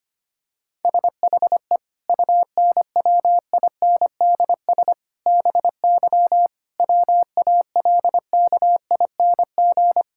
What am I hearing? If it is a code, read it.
Morse code, heard as a keyed tone, about 25 wpm, 700 Hz. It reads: SHE UNWINDS BY WALKING
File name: practice / 25wpm / Difficult Sentences with Hints - 25wpm